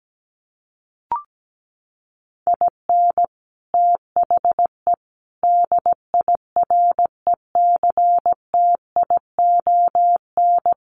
Morse code, heard as a keyed tone, about 17 wpm, 700 Hz. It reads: IN THE DIRECTION
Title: Sets of 5 Words 17wpm